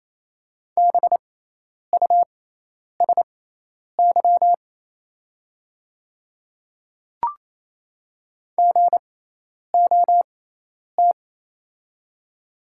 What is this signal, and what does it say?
Morse code, heard as a keyed tone, about 28 wpm, spaced as for 10 wpm, 700 Hz. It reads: BUSY GOT
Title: 6x Character Spacing - Top 500 Words - 28wpm